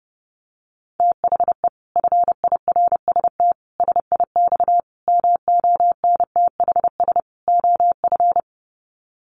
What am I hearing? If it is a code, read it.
Morse code, heard as a keyed tone, about 28 wpm, 700 Hz. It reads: THE FIRST SIX MONTHS OF
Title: Sets of 5 Words Encore2 30wpm